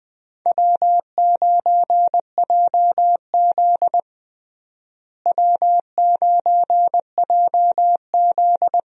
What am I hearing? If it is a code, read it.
Morse code, heard as a keyed tone, about 20 wpm, 700 Hz. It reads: W9JZ W9JZ
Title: USA Call Signs - 4 Characters - Repeated 2x - 20wpm